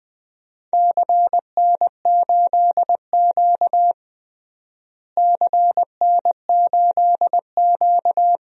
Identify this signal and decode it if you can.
Morse code, heard as a keyed tone, about 20 wpm, 700 Hz. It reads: CN8Q CN8Q
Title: DXCC-100 Call Signs - 4 Characters - Repeated 2x - 20wpm